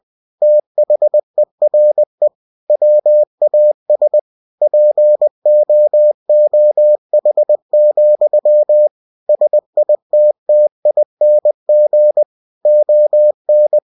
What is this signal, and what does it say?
Morse code, heard as a keyed tone, about 20 wpm, 600 Hz. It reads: THERE WAS POOH, SITTING ON
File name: cw-books / winnie-the-pooh / winnie-20wpm-page-0094